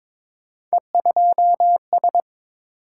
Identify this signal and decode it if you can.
Morse code, heard as a keyed tone, about 22 wpm, 700 Hz. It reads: E2S
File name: practice / 22wpm / Three Letter-Number Combinations - 22wpm